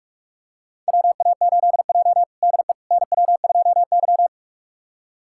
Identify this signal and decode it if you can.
Morse code, heard as a keyed tone, about 45 wpm, 700 Hz. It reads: WA8J DE NW2Y
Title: USA Callsign DE Callsign - 4 Characters - 45wpm